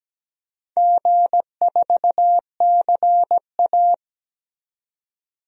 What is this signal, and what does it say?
Morse code, heard as a keyed tone, about 17 wpm, 700 Hz. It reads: G4CA
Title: DXCC-100 Call Signs - 4 Characters - 17wpm